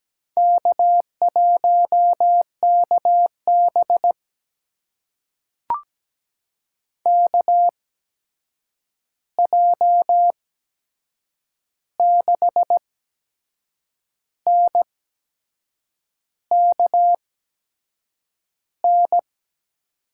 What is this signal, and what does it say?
Morse code, heard as a keyed tone, about 17 wpm, spaced as for 6 wpm, 700 Hz. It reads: K1KB KJ6NKN
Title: USA Call Signs - 8x Character Spacing - 17wpm